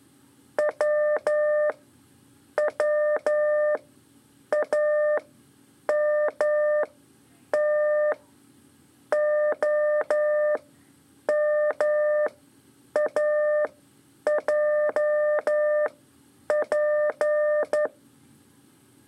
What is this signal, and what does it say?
Morse code, heard as a keyed tone, about 11 wpm, spaced as for 7 wpm, 600 Hz. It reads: WWAMTOMAJP
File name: 3 lezione CW 2 AIR